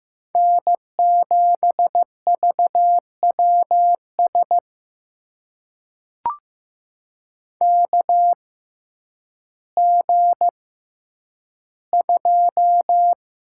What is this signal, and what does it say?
Morse code, heard as a keyed tone, about 15 wpm, spaced as for 9 wpm, 700 Hz. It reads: N7VWS KG2
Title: USA Call Signs - 6x Character Spacing - 15wpm